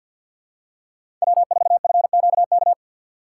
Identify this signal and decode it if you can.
Morse code, heard as a keyed tone, about 53 wpm, 700 Hz. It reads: W4FQK
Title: USA Call Signs - 5 Characters - 8x Character Spacing - 50wpm